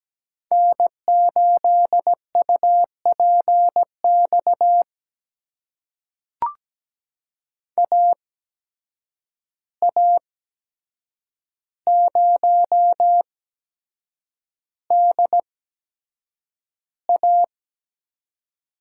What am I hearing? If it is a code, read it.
Morse code, heard as a keyed tone, about 17 wpm, spaced as for 7 wpm, 700 Hz. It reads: N8UPX AA0DA
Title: USA Call Signs - 5 Characters - 8x Character Spacing - 17wpm